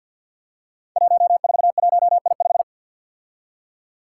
Morse code, heard as a keyed tone, about 53 wpm, 700 Hz. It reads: 141I5
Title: Five Challenging Letter-Number Combinations 50wpm